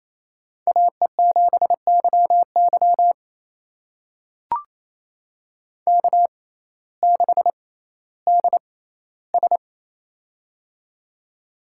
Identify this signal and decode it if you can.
Morse code, heard as a keyed tone, about 28 wpm, spaced as for 15 wpm, 700 Hz. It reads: AE7YY K6DS
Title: USA Call Signs - 6x Character Spacing - 28wpm